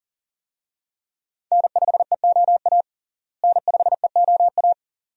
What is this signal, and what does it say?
Morse code, heard as a keyed tone, about 40 wpm, 700 Hz. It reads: N5EOA N5EOA
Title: USA Call Signs - 5 Characters - Repeated 2x - 40wpm